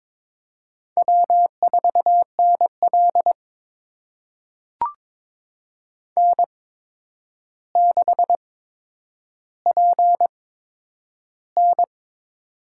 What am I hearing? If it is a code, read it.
Morse code, heard as a keyed tone, about 21 wpm, spaced as for 9 wpm, 700 Hz. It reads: W4NL N6PN
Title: USA Call Signs - 4 Characters - 8x Character Spacing - 22wpm